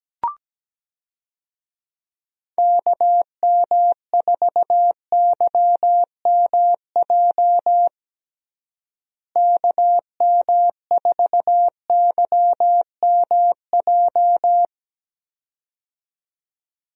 Morse code, heard as a keyed tone, about 17 wpm, 700 Hz. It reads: KM4YMJ KM4YMJ
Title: USA Call Signs - 6 Characters - Repeated 2x - 17wpm